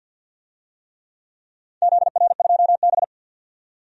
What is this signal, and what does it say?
Morse code, heard as a keyed tone, about 46 wpm, 700 Hz. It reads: ZR2B
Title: DXCC-100 Call Signs - 4 Characters - 50wpm